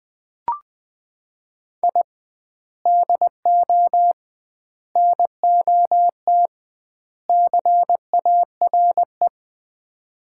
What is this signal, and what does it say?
Morse code, heard as a keyed tone, about 20 wpm, 700 Hz. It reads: I DO NOT CARE
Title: 2x Word-Spacing - Sets of 4 Words - 20wpm